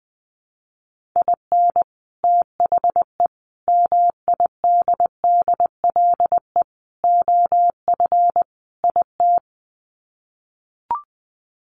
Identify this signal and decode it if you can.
Morse code, heard as a keyed tone, about 20 wpm, 700 Hz. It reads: IN THE MIDDLE OF IT
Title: Sets of 5 Words Encore 20wpm